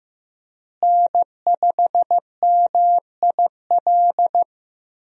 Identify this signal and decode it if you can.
Morse code, heard as a keyed tone, about 15 wpm, 700 Hz. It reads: N5MIL
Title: USA Call Signs - 5 Characters - Repeated 3x - 15wpm